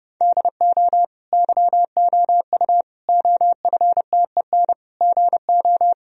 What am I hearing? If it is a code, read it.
Morse code, heard as a keyed tone, about 32 wpm, 700 Hz. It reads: DO YOU OFTEN GO
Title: Sentences from Top 1000 Words - Review Part 1 - 30wpm